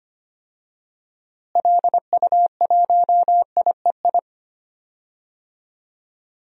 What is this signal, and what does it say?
Morse code, heard as a keyed tone, about 25 wpm, 700 Hz. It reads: LU1IEI
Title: DXCC-100 Call Signs - 6 Characters - Repeated 2x - 25wpm